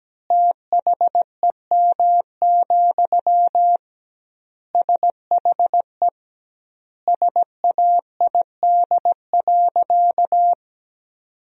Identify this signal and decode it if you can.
Morse code, heard as a keyed tone, about 17 wpm, 700 Hz. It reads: THEM, SHE SAID.
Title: 2x Word-Spacing - Sentences from Top 400 Words - 17wpm